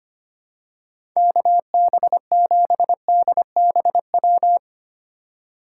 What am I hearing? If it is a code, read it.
Morse code, heard as a keyed tone, about 25 wpm, 700 Hz. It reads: KB7DBW
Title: USA Call Signs - 6 Characters - 3x Character Spacing - 25wpm